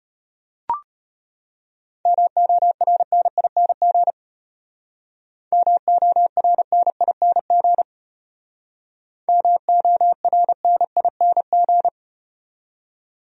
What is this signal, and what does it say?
Morse code, heard as a keyed tone, about 36 wpm, 700 Hz. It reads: MORNING MORNING MORNING